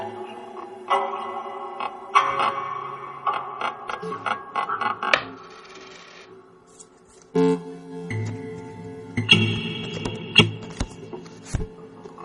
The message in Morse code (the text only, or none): none